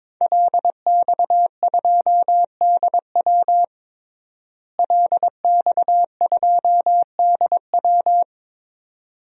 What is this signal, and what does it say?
Morse code, heard as a keyed tone, about 22 wpm, 700 Hz. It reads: LX2DW LX2DW
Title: DXCC-100 Call Signs - 5 Characters - Repeated 2x - 22wpm